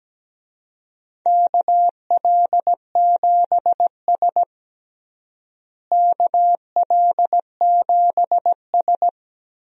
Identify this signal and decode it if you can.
Morse code, heard as a keyed tone, about 17 wpm, 700 Hz. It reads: KL7S KL7S
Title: DXCC-100 Call Signs - 4 Characters - Repeated 3x - 17wpm